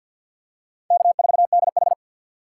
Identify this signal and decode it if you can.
Morse code, heard as a keyed tone, about 50 wpm, 700 Hz. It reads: K4DH
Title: USA Call Signs - 4 Characters - 6x Character Spacing - 50wpm